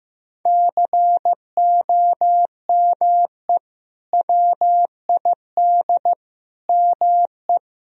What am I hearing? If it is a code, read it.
Morse code, heard as a keyed tone, about 15 wpm, 700 Hz. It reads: COME WID ME
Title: Sentences from Top 200 Words and Abbreviations 15wpm